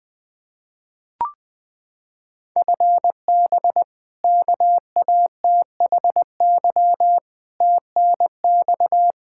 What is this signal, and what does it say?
Morse code, heard as a keyed tone, about 20 wpm, 700 Hz. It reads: FB KATHY TNX
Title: Practice QSO - Exchange 3 - More Info - 20wpm